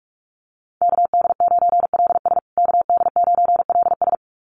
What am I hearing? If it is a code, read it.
Morse code, heard as a keyed tone, about 44 wpm, 690 Hz. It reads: KD9LS KD9LS
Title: USA Call Signs Repeated 2x 45wpm